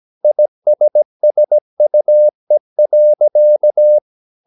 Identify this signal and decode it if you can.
Morse code, heard as a keyed tone, about 17 wpm, 600 Hz. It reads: ISSUE.
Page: Romanen Never af Ken Follet på engelsk.